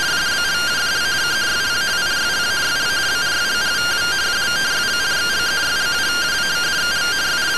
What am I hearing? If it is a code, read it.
Two-tone teleprinter signal, about 170 Hz shift.